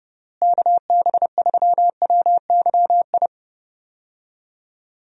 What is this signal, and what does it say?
Morse code, heard as a keyed tone, about 32 wpm, 700 Hz. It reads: KB3WYI
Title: USA Call Signs - 6 Characters - Repeated 3x - 30wpm